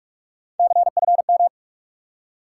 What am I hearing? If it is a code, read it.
Morse code, heard as a keyed tone, about 44 wpm, 700 Hz. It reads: CFM
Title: Top 180 QSO Elements 45wpm